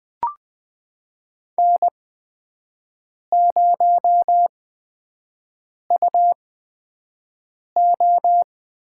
Morse code, heard as a keyed tone, about 21 wpm, spaced as for 6 wpm, 700 Hz. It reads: N0UO
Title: USA Call Signs - 8x Character Spacing - 20wpm